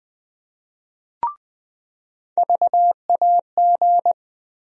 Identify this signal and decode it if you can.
Morse code, heard as a keyed tone, about 20 wpm, 700 Hz. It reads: VAG
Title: USA Call Sign Postfixes - 3 Letters 20wpm